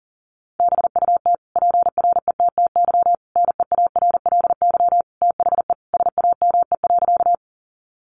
Morse code, heard as a keyed tone, about 40 wpm, 700 Hz. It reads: BUT PRETTY NEARLY THE SAME.
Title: Sentences from Top 1300 Words 40wpm